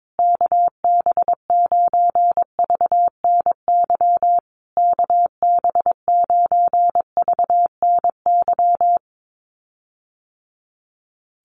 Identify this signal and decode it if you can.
Morse code, heard as a keyed tone, about 22 wpm, 700 Hz. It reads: KB9VNY KB9VNY
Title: USA Call Signs Repeated 3x 22wpm